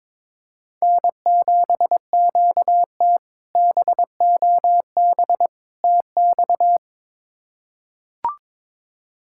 Morse code, heard as a keyed tone, about 22 wpm, 700 Hz. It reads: N7QT BOB TX
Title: K1USN SST QSO Practice - 22wpm